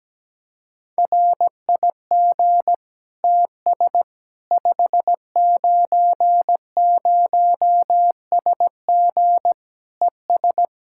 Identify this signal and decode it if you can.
Morse code, heard as a keyed tone, about 17 wpm, 700 Hz. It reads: RIG TS 590SG ES
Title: Practice QSO - Exchange 2 - Info - 17wpm